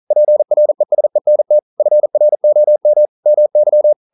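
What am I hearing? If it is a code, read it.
Morse code, heard as a keyed tone, about 40 wpm, 600 Hz. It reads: PRESENT FROM MY
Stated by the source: Morse code practice at 41 wpm, from CW med Gnister.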